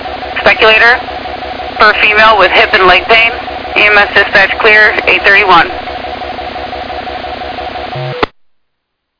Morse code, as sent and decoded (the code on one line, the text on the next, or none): none